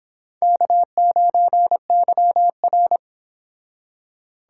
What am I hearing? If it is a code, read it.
Morse code, heard as a keyed tone, about 30 wpm, 690 Hz. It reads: K9YR